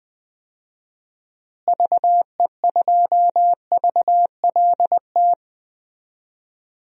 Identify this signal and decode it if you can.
Morse code, heard as a keyed tone, about 20 wpm, 700 Hz. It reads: VE2VLT
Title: DXCC-100 Call Signs - 6 Characters - Repeated 2x - 20wpm